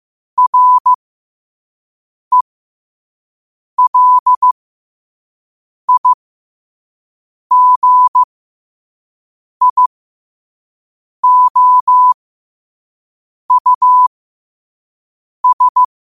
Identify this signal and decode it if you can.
Morse code, heard as a keyed tone, about 15 wpm, spaced as for 5 wpm, 1000 Hz. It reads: RELIGIOUS
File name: qotdfiles / 5wpm